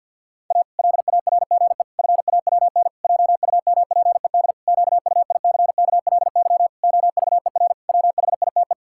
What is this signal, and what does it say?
Morse code, heard as a keyed tone, about 53 wpm, 700 Hz. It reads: A LARGE FAWN JUMPED QUICKLY OVER WHITE
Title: Pangram Challenge v2 - 50wpm